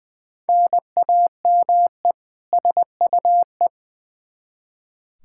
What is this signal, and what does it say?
Morse code, heard as a keyed tone, about 20 wpm, 700 Hz. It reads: NAME SUE
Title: Practice QSO - Exchange 1 - Beginning - 20wpm